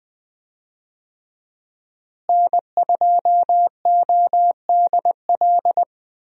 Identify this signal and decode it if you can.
Morse code, heard as a keyed tone, about 20 wpm, 700 Hz. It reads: N2ODL
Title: USA Call Signs - 5 Characters - Repeated 2x - 20wpm